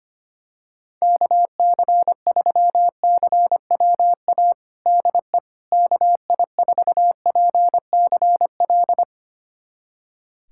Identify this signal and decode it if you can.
Morse code, heard as a keyed tone, about 25 wpm, 700 Hz. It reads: KC3CWA DE KI4PCL
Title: USA Callsign DE Callsign 25wpm